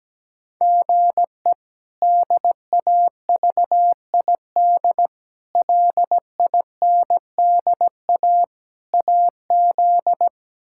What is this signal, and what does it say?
Morse code, heard as a keyed tone, about 17 wpm, 700 Hz. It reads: GE DAVID LINDA AZ
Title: K1USN SST QSO Practice - 17wpm